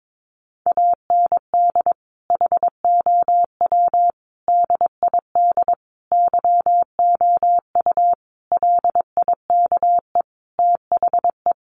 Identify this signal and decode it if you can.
Morse code, heard as a keyed tone, about 22 wpm, 700 Hz. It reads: AND HOW DID YOU LIKE THE
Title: Sentences from Top 1300 Words 22wpm